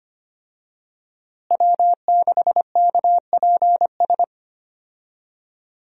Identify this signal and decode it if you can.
Morse code, heard as a keyed tone, about 25 wpm, 700 Hz. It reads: W6KPS
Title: USA Call Signs - 5 Characters - Repeated 2x - 25wpm